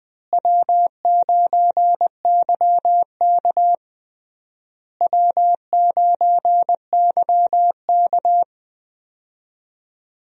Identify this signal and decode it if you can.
Morse code, heard as a keyed tone, about 20 wpm, 700 Hz. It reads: W9YK W9YK
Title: USA Call Signs - 4 Characters - Repeated 3x - 20wpm